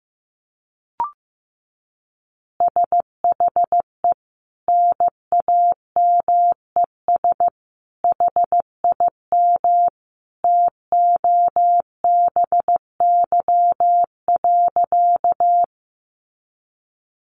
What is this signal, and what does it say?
Morse code, heard as a keyed tone, about 15 wpm, 700 Hz. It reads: SHE NAMES HIM TOBY.
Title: Easy Phrases 15wpm